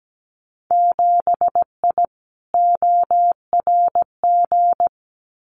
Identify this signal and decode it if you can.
Morse code, heard as a keyed tone, about 17 wpm, 700 Hz. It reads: 7I ORG
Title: Winter Field Day Exchanges Encore 17wpm